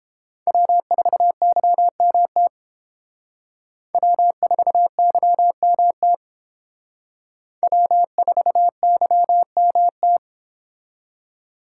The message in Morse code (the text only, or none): W4YMT W4YMT W4YMT